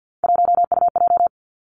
Morse code, heard as a keyed tone, about 50 wpm, 700 Hz. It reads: 1UJ
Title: Three Challenging Letter-Number Combinations 50wpm